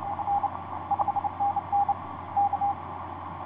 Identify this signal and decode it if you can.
Morse code, heard as a keyed tone, about 27 wpm, 810 Hz, with noise. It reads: N 5NN K
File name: ys1zc ik2cik etc 30m cw 07feb2010 2249